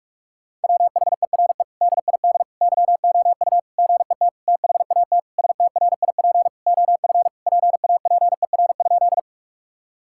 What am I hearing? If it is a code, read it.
Morse code, heard as a keyed tone, about 46 wpm, 700 Hz. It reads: WHERE DID YOU GET THAT STRIP OF PAPER?
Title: Sentences from Top 1600 Words 45wpm